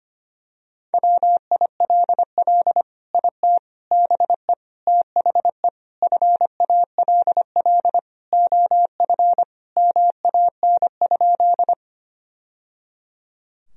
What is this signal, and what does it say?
Morse code, heard as a keyed tone, about 25 wpm, 700 Hz. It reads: WILL IT BE THE FALL OF MAN?
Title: Sentences from Top 1000 Words - Review Part 2 - 25wpm